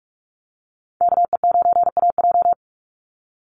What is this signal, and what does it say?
Morse code, heard as a keyed tone, about 44 wpm, 690 Hz. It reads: KE9AJ
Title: NA SOTA Chaser Callsigns 45wpm